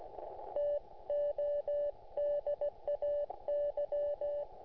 Morse code, heard as a keyed tone, about 16 wpm, 580 Hz. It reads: TODAY